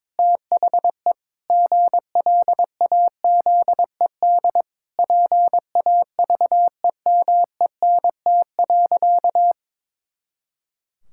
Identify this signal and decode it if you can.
Morse code, heard as a keyed tone, about 22 wpm, 700 Hz. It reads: THE GLAZED PAVEMENT.
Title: Pangram Challenge v2 - 22wpm